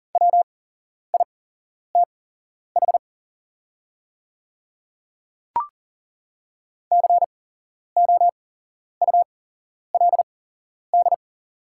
Morse code, heard as a keyed tone, about 40 wpm, spaced as for 11 wpm, 700 Hz. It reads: WITH COULD
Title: 8x Character Spacing - Top 100 Words - 40wpm